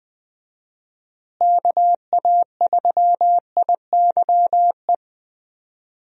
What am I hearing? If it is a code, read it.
Morse code, heard as a keyed tone, about 20 wpm, 700 Hz. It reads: KA3IYE
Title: USA Call Signs - 6 Characters - Repeated 2x - 20wpm